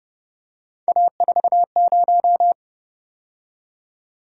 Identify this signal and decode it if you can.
Morse code, heard as a keyed tone, about 29 wpm, 700 Hz. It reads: A40